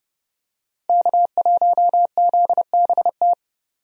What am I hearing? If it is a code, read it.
Morse code, heard as a keyed tone, about 30 wpm, 700 Hz. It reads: K1ZBT
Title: USA Call Signs - 5 Characters - 30wpm